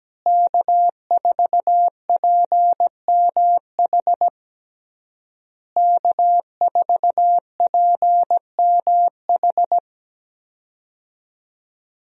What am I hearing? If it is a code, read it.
Morse code, heard as a keyed tone, about 17 wpm, 700 Hz. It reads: K4PMH K4PMH
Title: USA Call Signs - 5 Characters - Repeated 3x - 17wpm